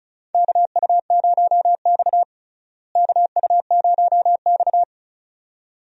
Morse code, heard as a keyed tone, about 34 wpm, 700 Hz. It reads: KU0X KU0X
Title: USA Call Signs - 4 Characters - Repeated 2x - 35wpm